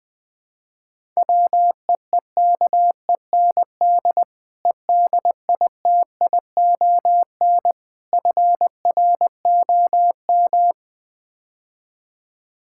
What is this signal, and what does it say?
Morse code, heard as a keyed tone, about 21 wpm, 700 Hz. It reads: WEEKEND EDITION FROM
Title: Build A Phrase - 5-Word Phrases - 20wpm